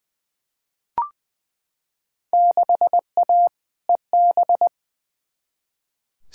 Morse code, heard as a keyed tone, about 20 wpm, 700 Hz. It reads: 6A EB
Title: Field Day Exchanges Encore - 20wpm